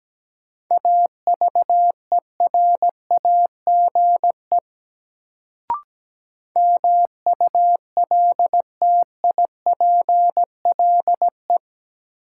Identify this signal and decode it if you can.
Morse code, heard as a keyed tone, about 17 wpm, 700 Hz. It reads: AVERAGE MULTIPLE
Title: Sets of 5 Words - New Word Drill - 17wpm